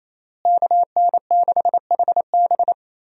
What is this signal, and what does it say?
Morse code, heard as a keyed tone, about 27 wpm, 700 Hz. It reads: KN6HB
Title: USA Call Signs - 5 Characters - Repeated 3x - 28wpm